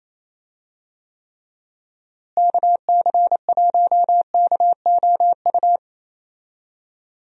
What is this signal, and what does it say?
Morse code, heard as a keyed tone, about 28 wpm, 700 Hz. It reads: KC1KOU
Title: USA Call Signs - 6 Characters - Repeated 3x - 28wpm